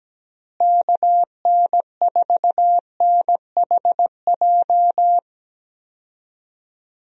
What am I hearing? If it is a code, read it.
Morse code, heard as a keyed tone, about 17 wpm, 700 Hz. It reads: KN4NHJ
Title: USA Call Signs - 6 Characters - Repeated 2x - 17wpm